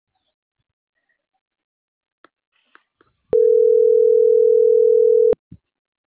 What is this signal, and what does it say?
Ringback tone (North America), 440 + 480 Hz.